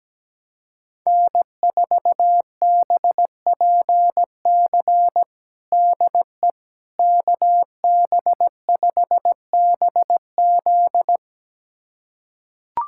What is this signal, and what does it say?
Morse code, heard as a keyed tone, about 17 wpm, 700 Hz. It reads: N4BPC DE KB5BZ
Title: USA Callsign DE Callsign - 5 Characters - 17wpm